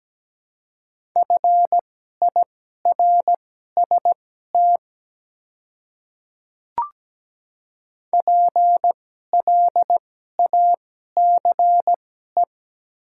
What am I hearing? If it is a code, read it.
Morse code, heard as a keyed tone, about 17 wpm, spaced as for 12 wpm, 700 Hz. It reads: FIRST PLACE
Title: 2x Character Spacing - Top 200 Words - 17wpm